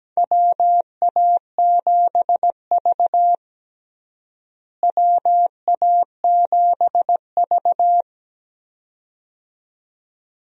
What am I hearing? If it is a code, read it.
Morse code, heard as a keyed tone, about 17 wpm, 700 Hz. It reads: WA7V WA7V
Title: USA Call Signs - 4 Characters - Repeated 2x - 17wpm